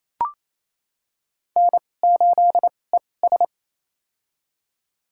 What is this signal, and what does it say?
Morse code, heard as a keyed tone, about 28 wpm, spaced as for 20 wpm, 700 Hz. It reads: N8ES
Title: USA Call Signs - 4 Characters - 2x Character Spacing - 28wpm